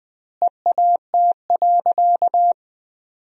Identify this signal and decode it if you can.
Morse code, heard as a keyed tone, about 20 wpm, 700 Hz. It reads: EAT.
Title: 2x Word-Spacing - Sentences from Top 300 Words - 20wpm